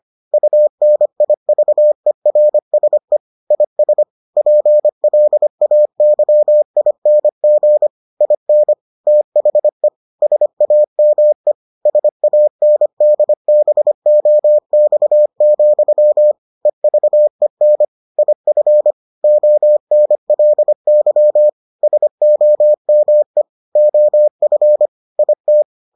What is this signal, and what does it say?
Morse code, heard as a keyed tone, about 25 wpm, 600 Hz. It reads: UNIVERSE IS PLAYING IN THE SAME SANDBOX, EVEN IF ONLY SOME OF IT